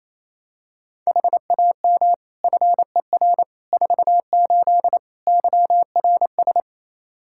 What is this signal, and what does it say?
Morse code, heard as a keyed tone, about 28 wpm, 700 Hz. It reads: HAM FER 48 YRS
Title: Practice QSO - Exchange 3 - More Info - 28wpm